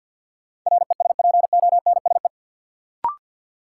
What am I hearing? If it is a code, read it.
Morse code, heard as a keyed tone, about 53 wpm, 700 Hz. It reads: RESPONSE
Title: Sentences from Top 1400 Words - New Word Drill Alt - 50wpm